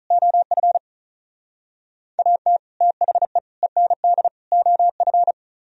Morse code, heard as a keyed tone, about 47 wpm, 700 Hz. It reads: OF AT THE END OF